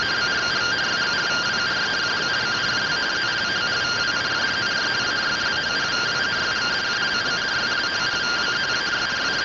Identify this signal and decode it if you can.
Two-tone teleprinter signal, about 170 Hz shift.
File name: findings / SAGITTAL 43mm 4mm.gen-15 rtty rx